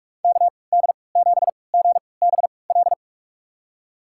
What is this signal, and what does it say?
Morse code, heard as a keyed tone, about 46 wpm, spaced as for 26 wpm, 700 Hz. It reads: KD7GBL
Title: USA Call Signs - 6 Characters - 3x Character Spacing - 45wpm